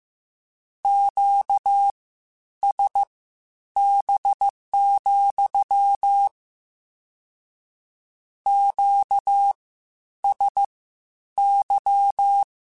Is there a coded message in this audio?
Yes: Morse code from 0.8 to 12.4 s, heard as a keyed tone, about 15 wpm, spaced as for 9 wpm, 800 Hz. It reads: QSB, QSY